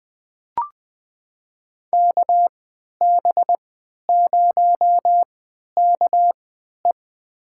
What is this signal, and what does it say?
Morse code, heard as a keyed tone, about 20 wpm, spaced as for 11 wpm, 700 Hz. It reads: KB0KE
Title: USA Call Signs - 3x Character Spacing - 20wpm